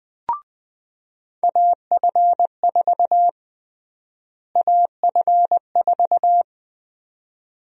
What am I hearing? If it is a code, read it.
Morse code, heard as a keyed tone, about 20 wpm, 700 Hz. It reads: AF4 AF4
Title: USA Call Sign Prefixes - Repeated 2x - 20wpm